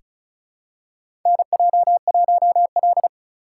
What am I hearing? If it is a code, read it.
Morse code, heard as a keyed tone, about 36 wpm, 700 Hz. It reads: NJ1L
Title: USA Call Signs - 4 Characters - 2x Character Spacing - 35wpm